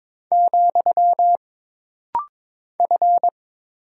Morse code, heard as a keyed tone, about 22 wpm, 700 Hz. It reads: , F
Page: This rapid-fire timing is geared to push your ICR (Instant Character Recognition) skills.